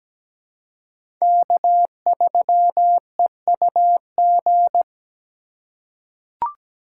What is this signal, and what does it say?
Morse code, heard as a keyed tone, about 17 wpm, 700 Hz. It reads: K3EUG
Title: USA Call Signs - 5 Characters - 3x Character Spacing - 17wpm